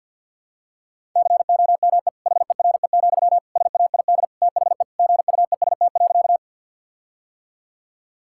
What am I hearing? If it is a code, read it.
Morse code, heard as a keyed tone, about 50 wpm, 700 Hz. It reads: COME HERE, SAID THE GUEST.
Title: Sentences from Top 1600 Words 50wpm